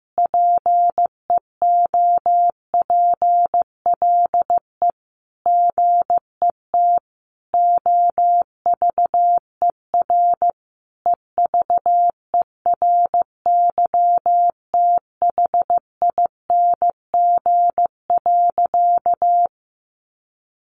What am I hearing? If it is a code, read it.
Morse code, heard as a keyed tone, about 15 wpm, 700 Hz. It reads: PEOPLE GET OVER EVERYTHING.
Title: Sentences from Top 900 Words 15wpm